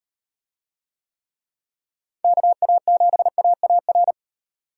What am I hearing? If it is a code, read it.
Morse code, heard as a keyed tone, about 38 wpm, 700 Hz. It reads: KA7AAR